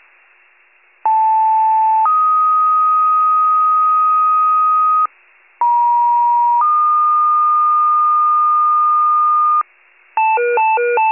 Two-tone page, heard as a one-second tone, then a three-second tone, about 880 then 1250 Hz, about 950 then 1220 Hz. It is heard from 1.1 to 9.6 s.
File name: wcalert